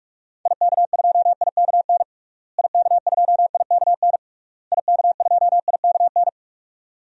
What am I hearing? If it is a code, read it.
Morse code, heard as a keyed tone, about 46 wpm, 700 Hz. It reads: IK2IKN IK2IKN IK2IKN
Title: DXCC-100 Call Signs - 6 Characters - Repeated 3x - 45wpm